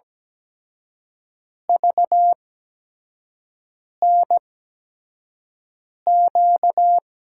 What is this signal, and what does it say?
Morse code, heard as a keyed tone, about 17 wpm, spaced as for 5 wpm, 700 Hz. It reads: VNQ
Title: USA Call Signs - 6 Characters - 8x Character Spacing - 17wpm